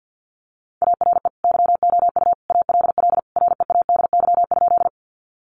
Morse code, heard as a keyed tone, about 50 wpm, 700 Hz. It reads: ARE YOU ALL READY?
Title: Sentences from Top 400 Words 50wpm